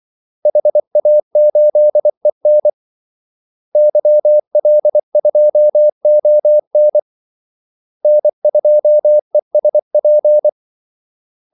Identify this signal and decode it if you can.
Morse code, heard as a keyed tone, about 24 wpm, 600 Hz. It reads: HA8EN YL2ON N2ESP
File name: Callsigns-24wpm